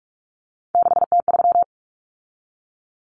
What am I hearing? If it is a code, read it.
Morse code, heard as a keyed tone, about 44 wpm, 700 Hz. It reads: 6T3
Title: Common Cut Numbers - 3-Digit 45wpm